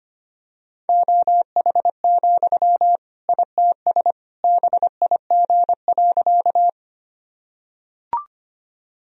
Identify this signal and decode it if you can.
Morse code, heard as a keyed tone, about 25 wpm, 700 Hz. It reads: OH, ITS BIG.
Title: Sentences from Top 1000 Words - Review Part 1 - 25wpm